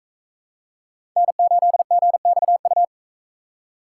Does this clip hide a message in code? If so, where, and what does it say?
Yes: Morse code from 1.2 to 2.8 s, heard as a keyed tone, about 44 wpm, 700 Hz. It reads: N8GXU